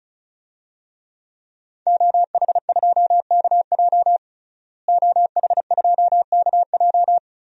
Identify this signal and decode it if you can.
Morse code, heard as a keyed tone, about 34 wpm, 700 Hz. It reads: OH2KJ OH2KJ
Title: DXCC-100 Call Signs - 5 Characters - Repeated 2x - 35wpm